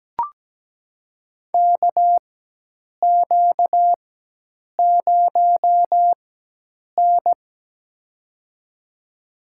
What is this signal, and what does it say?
Morse code, heard as a keyed tone, about 17 wpm, spaced as for 8 wpm, 700 Hz. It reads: KQ0N
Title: USA Call Signs - 4 Characters - 4x Character Spacing - 17wpm